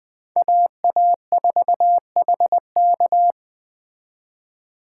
Morse code, heard as a keyed tone, about 20 wpm, 700 Hz. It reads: AA4HK
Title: Speed Racing - USA Call Signs 17wpm